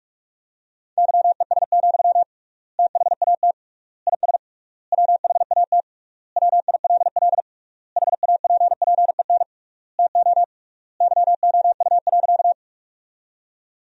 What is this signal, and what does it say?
Morse code, heard as a keyed tone, about 46 wpm, 700 Hz. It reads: YES, THAT IS WHAT WILL HAPPEN TO YOU.
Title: 3x Word-Spacing - Sentences from Top 500 Words - 45wpm